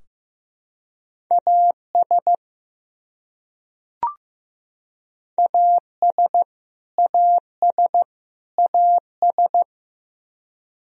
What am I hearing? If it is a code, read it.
Morse code, heard as a keyed tone, about 15 wpm, 700 Hz. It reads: AS AS AS AS
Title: Top 100 Words Repeated 15wpm